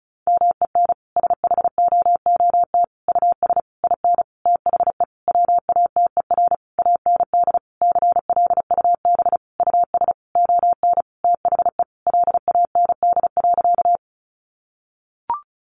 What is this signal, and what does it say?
Morse code, heard as a keyed tone, about 34 wpm, 700 Hz. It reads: MEN SHOOT US IN THE WATER AND CLUB US ON THE LAND.
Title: Sentences from Top 1200 Words 35wpm